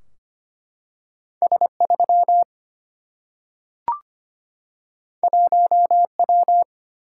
Morse code, heard as a keyed tone, about 25 wpm, 700 Hz. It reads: S3 1W
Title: Single-Letter Single-Number 25wpm